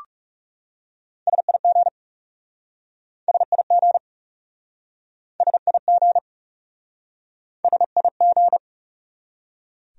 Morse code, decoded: SIG SIG SIG SIG

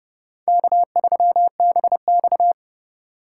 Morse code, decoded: K3BX

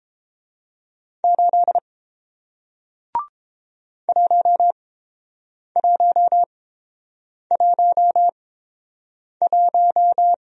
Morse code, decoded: 8 1111

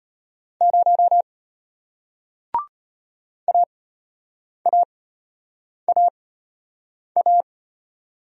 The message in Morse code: 0 AAAA

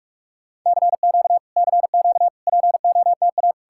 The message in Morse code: CQ CQ POTA

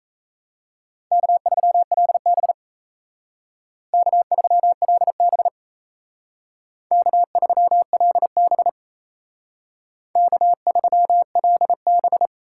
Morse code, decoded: K3LB K3LB K3LB K3LB

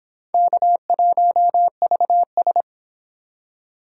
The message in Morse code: K1VS

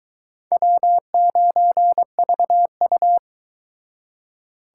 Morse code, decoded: W9VU

